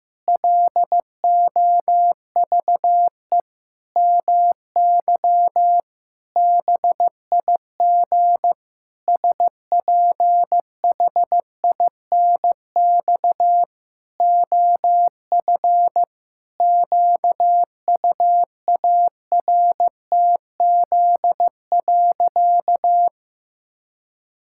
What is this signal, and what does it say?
Morse code, heard as a keyed tone, about 16 wpm, 700 Hz. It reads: LOVE MY BIG SPHINX OF QUARTZ.
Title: Pangram Challenge v2 - 15wpm